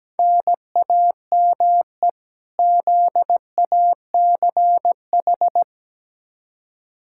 Morse code, heard as a keyed tone, about 17 wpm, 700 Hz. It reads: NAME ZACH
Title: K1USN SST QSO Practice 17wpm